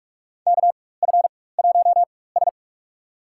Morse code, heard as a keyed tone, about 46 wpm, spaced as for 21 wpm, 700 Hz. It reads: KF1S
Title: USA Call Signs - 4 Characters - 4x Character Spacing - 45wpm